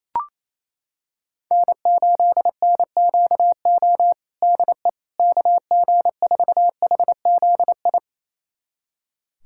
Morse code, heard as a keyed tone, about 28 wpm, 700 Hz. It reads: N8NQO DE KG4HZI
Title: USA Callsign DE Callsign 28wpm